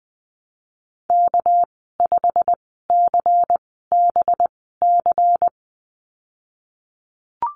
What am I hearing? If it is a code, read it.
Morse code, heard as a keyed tone, about 20 wpm, spaced as for 15 wpm, 700 Hz. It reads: K5CBC
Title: 219 Lesson 35g - Number 0 - Call Signs 20wpm